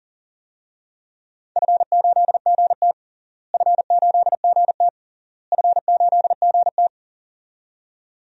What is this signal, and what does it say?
Morse code, heard as a keyed tone, about 40 wpm, 700 Hz. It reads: F8GT F8GT F8GT
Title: DXCC-100 Call Signs - 4 Characters - Repeated 3x - 40wpm